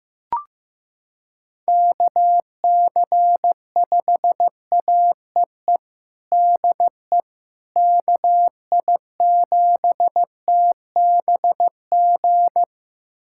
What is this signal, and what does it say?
Morse code, heard as a keyed tone, about 15 wpm, 700 Hz. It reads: KC5AEE DE KI7TBG
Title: USA Callsign DE Callsign - 6 Characters - 15wpm